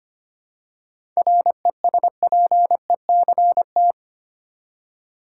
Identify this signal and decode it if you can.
Morse code, heard as a keyed tone, about 25 wpm, 700 Hz. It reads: RESPECT
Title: Seven Letter Words - 25wpm